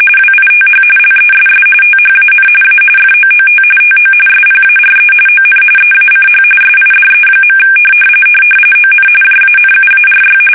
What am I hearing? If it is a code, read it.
RTTY, about 45.45 baud. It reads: WITH LESS THAN A FOOT TO SPARE ON EACH SIDE. ON JANUARY
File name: fsk850 / battleshipnj